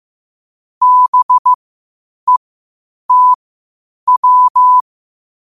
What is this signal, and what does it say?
Morse code, heard as a keyed tone, about 15 wpm, spaced as for 8 wpm, 1000 Hz. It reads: BETW
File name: qotdfiles / 8wpm